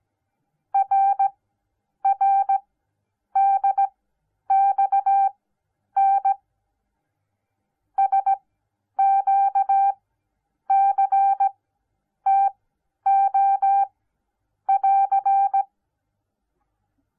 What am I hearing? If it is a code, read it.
Morse code, heard as a keyed tone, about 17 wpm, spaced as for 9 wpm, 790 Hz. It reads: RRDXN SQCTO<AR>